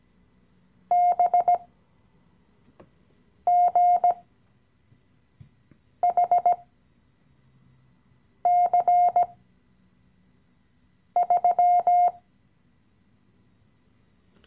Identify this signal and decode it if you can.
Morse code, heard as a keyed tone, about 17 wpm, spaced as for 4 wpm, 710 Hz. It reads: BGHC3
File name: CW-Online-Kurs / Anfänger / Lektion 015